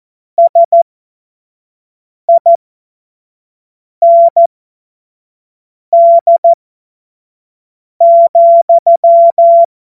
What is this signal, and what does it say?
Morse code, heard as a keyed tone, about 14 wpm, spaced as for 5 wpm, 680 Hz. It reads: SIND,
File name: Morsekurs / Witze / Witz 25. Juli 2023 25 F70